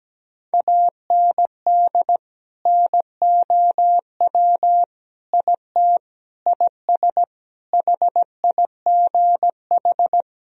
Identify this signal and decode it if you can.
Morse code, heard as a keyed tone, about 17 wpm, 700 Hz. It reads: AND NOW IT IS HIGH
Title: Sentences from Top 1000 Words - Review Part 1 - 17wpm